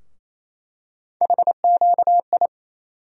Morse code, heard as a keyed tone, about 27 wpm, 700 Hz. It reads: HQI
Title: USA Call Sign Postfixes - 3 Letters 28wpm